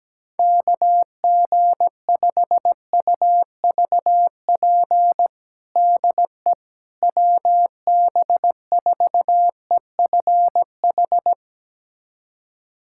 Morse code, heard as a keyed tone, about 17 wpm, 700 Hz. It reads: KG5UVP DE WB4EFH
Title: USA Callsign DE Callsign 17wpm